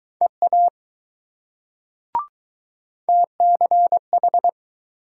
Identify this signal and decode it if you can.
Morse code, heard as a keyed tone, about 20 wpm, 700 Hz. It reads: EA TCH